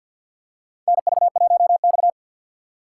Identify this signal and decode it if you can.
Morse code, heard as a keyed tone, about 53 wpm, 700 Hz. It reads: NV1X
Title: USA Call Signs - 4 Characters - 3x Character Spacing - 50wpm